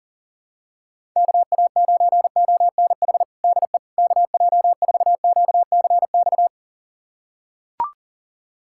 Morse code, heard as a keyed tone, about 37 wpm, 700 Hz. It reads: KA9ONH DE KJ4QCX
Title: USA Callsign DE Callsign - 6 Characters - 40wpm